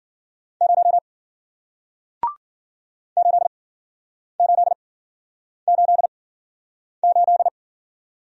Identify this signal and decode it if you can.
Morse code, heard as a keyed tone, about 53 wpm, spaced as for 9 wpm, 700 Hz. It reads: 0 8888